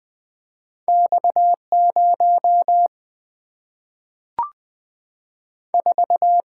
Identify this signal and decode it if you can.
Morse code, heard as a keyed tone, about 20 wpm, 700 Hz. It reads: X 0 4
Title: Single-Letter Single-Number 20wpm